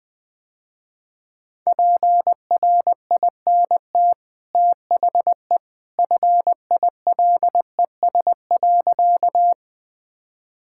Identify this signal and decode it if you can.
Morse code, heard as a keyed tone, about 21 wpm, 700 Hz. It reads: PRINT THE FILES.
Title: Easy Sentences with Hints - 20wpm